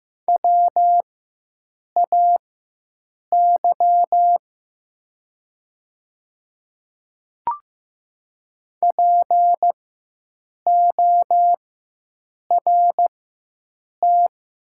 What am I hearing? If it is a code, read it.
Morse code, heard as a keyed tone, about 15 wpm, spaced as for 7 wpm, 700 Hz. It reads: WAY PORT
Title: 4x Character Spacing - Top 300 Words - 15wpm